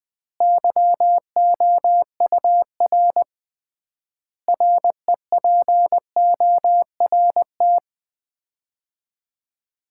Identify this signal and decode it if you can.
Morse code, heard as a keyed tone, about 21 wpm, 700 Hz. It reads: YOUR REPORT
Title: Common QSO Phrases - 3x Word-Spacing - 20wpm